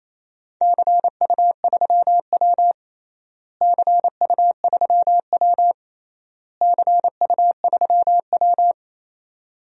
Morse code, heard as a keyed tone, about 28 wpm, 700 Hz. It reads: CU3W CU3W CU3W
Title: DXCC-100 Call Signs - 4 Characters - Repeated 3x - 28wpm